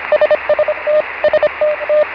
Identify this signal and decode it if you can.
Morse code, heard as a keyed tone, about 25 wpm, 620 Hz, with noise. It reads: SST STT
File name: RS 4-TELEMETRIE